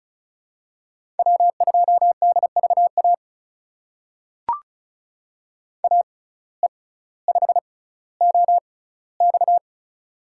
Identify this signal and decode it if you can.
Morse code, heard as a keyed tone, about 36 wpm, spaced as for 18 wpm, 700 Hz. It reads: W2DVA AE5OX
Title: USA Call Signs - 5 Characters - 6x Character Spacing - 35wpm